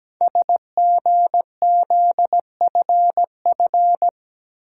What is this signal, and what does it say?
Morse code, heard as a keyed tone, about 17 wpm, 700 Hz. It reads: SGZFF
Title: Code Groups - 5 Letters - 17wpm